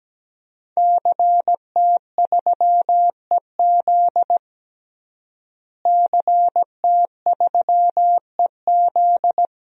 Morse code, heard as a keyed tone, about 17 wpm, 700 Hz. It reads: CT3EZ CT3EZ
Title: DXCC-100 Call Signs - 5 Characters - Repeated 3x - 17wpm